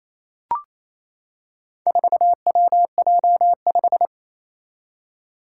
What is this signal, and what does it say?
Morse code, heard as a keyed tone, about 28 wpm, 700 Hz. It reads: 4WJ5
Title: Four Challenging Letter-Number Combinations 28wpm